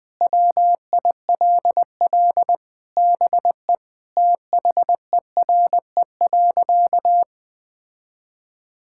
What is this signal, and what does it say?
Morse code, heard as a keyed tone, about 20 wpm, 700 Hz. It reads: WILL BE THERE.
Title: Sentences with 3-Word Phrases - Spoken Beginning - 20wpm